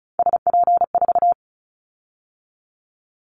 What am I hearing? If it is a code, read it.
Morse code, heard as a keyed tone, about 34 wpm, 700 Hz. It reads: SP4
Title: Callsign Prefixes for Top 25 DXCC - 35wpm